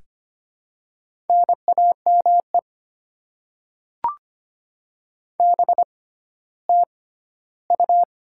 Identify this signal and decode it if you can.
Morse code, heard as a keyed tone, about 25 wpm, spaced as for 14 wpm, 700 Hz. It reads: NAME BTU
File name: Top 20 QSO Elements - 6x Character Spacing - 25wpm